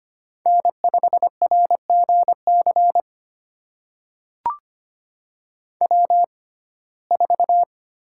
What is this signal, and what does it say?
Morse code, heard as a keyed tone, about 25 wpm, spaced as for 18 wpm, 700 Hz. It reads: N5RGC W4
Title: USA Call Signs - 6x Character Spacing - 25wpm